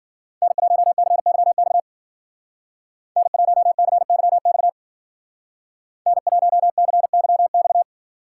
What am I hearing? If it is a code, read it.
Morse code, heard as a keyed tone, about 53 wpm, 700 Hz. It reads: N1CYX N1CYX N1CYX